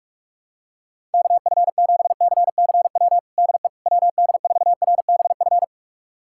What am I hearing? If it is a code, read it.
Morse code, heard as a keyed tone, about 46 wpm, 700 Hz. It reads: KF7CCW DE WD4RBF
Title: USA Callsign DE Callsign - 6 Characters - 45wpm